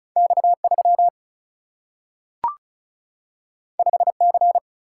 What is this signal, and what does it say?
Morse code, heard as a keyed tone, about 36 wpm, 700 Hz. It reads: X 3 5 C
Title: Single-Letter Single-Number 35wpm